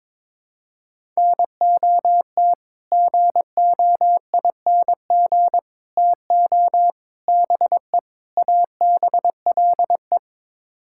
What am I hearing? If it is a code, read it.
Morse code, heard as a keyed tone, about 22 wpm, 700 Hz. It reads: NOT GOING TO BE ABLE
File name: practice / 22wpm / Build A Phrase - 5-Word Phrases - 22wpm